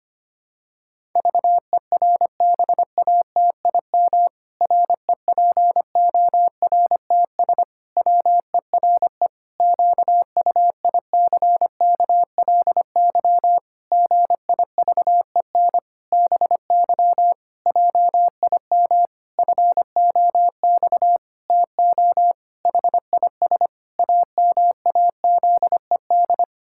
Morse code, heard as a keyed tone, about 25 wpm, 700 Hz. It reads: VERBATIM REPORTS WERE QUICKLY GIVEN BY JIM FOX TO HIS AMAZED
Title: Pangram Challenge v2 - 25wpm